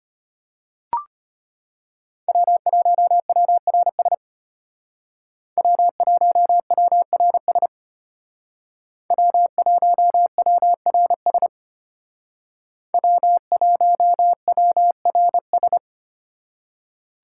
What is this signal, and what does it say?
Morse code, heard as a keyed tone, about 32 wpm, 700 Hz. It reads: W1WRS W1WRS W1WRS W1WRS